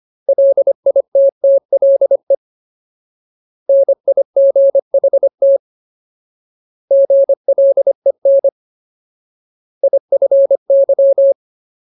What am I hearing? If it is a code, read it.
Morse code, heard as a keyed tone, about 25 wpm, 550 Hz. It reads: LITTLE NIGHT GLEN IFY
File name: Ultimate Word List / 25WPM Ultimate